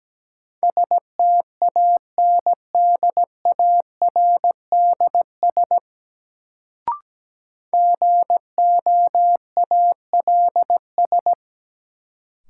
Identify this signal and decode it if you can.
Morse code, heard as a keyed tone, about 17 wpm, 700 Hz. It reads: STANDARDS GOALS
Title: Sentences from Top 1000 Words - Review Part 6 - New Word Drill - 17wpm